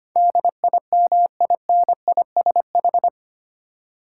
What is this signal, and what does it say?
Morse code, heard as a keyed tone, about 25 wpm, 700 Hz. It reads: DIMINISH
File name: Eight Letter Words - 25wpm